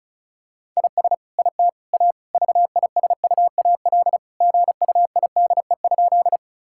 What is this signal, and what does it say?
Morse code, heard as a keyed tone, about 36 wpm, 700 Hz. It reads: IS IT A VISUAL GUIDE?
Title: Sentences from Top 1000 Words - Review Part 6 - 35wpm